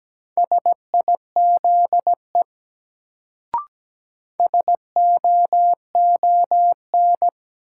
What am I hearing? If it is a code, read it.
Morse code, heard as a keyed tone, about 17 wpm, 700 Hz. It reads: SIZE SOON
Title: Sets of 3 Words - New Word Drill - 17wpm